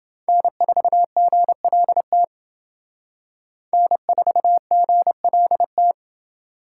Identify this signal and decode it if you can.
Morse code, heard as a keyed tone, about 28 wpm, 700 Hz. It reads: N4GLT N4GLT